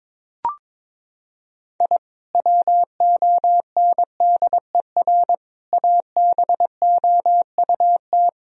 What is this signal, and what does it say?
Morse code, heard as a keyed tone, about 22 wpm, 700 Hz. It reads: I WONDER ABOUT
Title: Sentences from Top 1000 Words - Review Part 3 - 22wpm